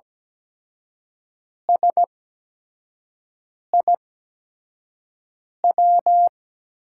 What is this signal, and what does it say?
Morse code, heard as a keyed tone, about 17 wpm, spaced as for 5 wpm, 700 Hz. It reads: SIW
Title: USA Call Signs - 5 Characters - 8x Character Spacing - 17wpm